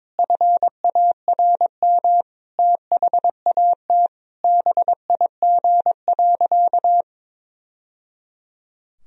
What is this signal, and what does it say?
Morse code, heard as a keyed tone, about 22 wpm, 700 Hz. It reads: FARM THAT BIG.
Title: Sentences from Top 1000 Words - Review Part 2 - 22wpm